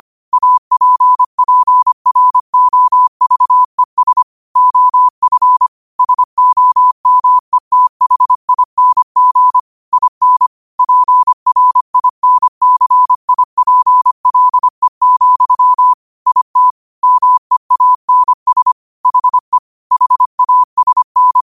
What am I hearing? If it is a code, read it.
Morse code, heard as a keyed tone, about 25 wpm, 1000 Hz. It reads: APPROVES OF SOMETHING IN PRINCIPLE, IT MEANS HE HASN
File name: qotdfiles / 25wpm